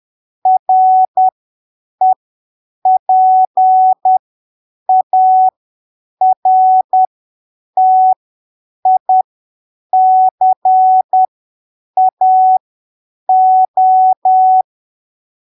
Morse code, heard as a keyed tone, about 10 wpm, spaced as for 7 wpm, 750 Hz. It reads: REPARTICAO
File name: arquivos / texto / texto 14 10ppm